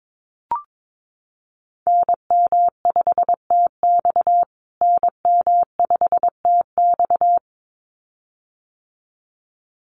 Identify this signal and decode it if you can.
Morse code, heard as a keyed tone, about 22 wpm, 700 Hz. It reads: NM5TX NM5TX
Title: USA Call Signs Repeated 2x 22wpm